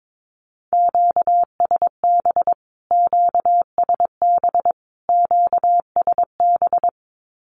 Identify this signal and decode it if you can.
Morse code, heard as a keyed tone, about 22 wpm, 700 Hz. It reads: QSB QSB QSB
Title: Q-Codes Repeated 3x 22wpm